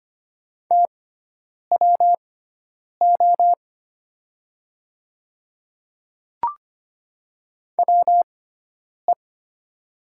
Morse code, heard as a keyed tone, about 25 wpm, spaced as for 9 wpm, 700 Hz. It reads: TWO WE
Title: 6x Character Spacing - Top 200 Words - 25wpm